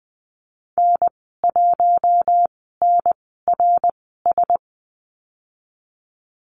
Morse code, heard as a keyed tone, about 20 wpm, spaced as for 14 wpm, 700 Hz. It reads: N1NRS
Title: 069 Lesson 12g - Letter d - Call Signs 20wpm